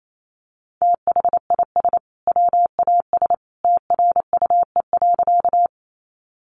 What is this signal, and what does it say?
Morse code, heard as a keyed tone, about 28 wpm, 700 Hz. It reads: THIS WAS TRUE.
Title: Sentences from Top 400 Words 28wpm